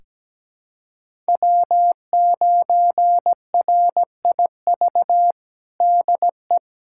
Morse code, heard as a keyed tone, about 17 wpm, 700 Hz. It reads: W9RIV DE
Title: USA Callsign DE Callsign - 5 Characters - 17wpm